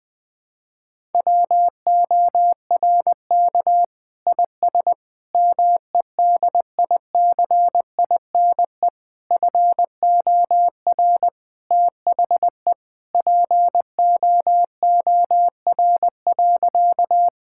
Morse code, heard as a keyed tone, about 20 wpm, 700 Hz. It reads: WORK IS MEDICINE FOR THE POOR.
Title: Sentences from Top 1000 Words - Review Part 6 - 20wpm